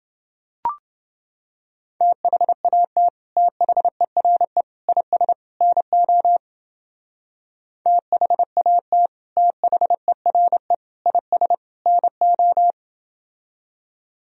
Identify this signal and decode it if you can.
Morse code, heard as a keyed tone, about 28 wpm, 700 Hz. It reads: THAT THERE IS NO THAT THERE IS NO